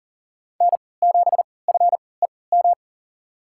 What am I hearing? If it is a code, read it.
Morse code, heard as a keyed tone, about 40 wpm, spaced as for 23 wpm, 700 Hz. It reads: N7FEM
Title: USA Call Signs - 3x Character Spacing - 40wpm